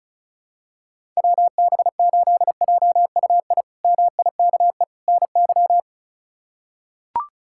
Morse code, heard as a keyed tone, about 34 wpm, 700 Hz. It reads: WB8JUI MIKE NY
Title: K1USN SST QSO Practice - 35wpm